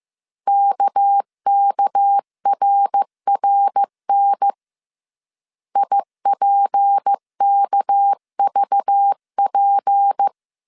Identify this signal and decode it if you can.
Morse code, heard as a keyed tone, about 15 wpm, 800 Hz. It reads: KKRRN IPKVP
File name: CW / DET / 1 6 1500C S C 14wpm 2951